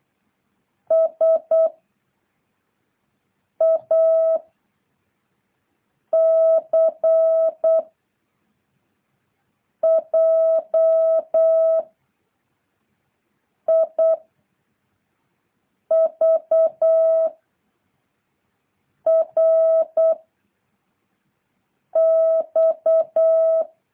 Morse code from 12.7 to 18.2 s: I V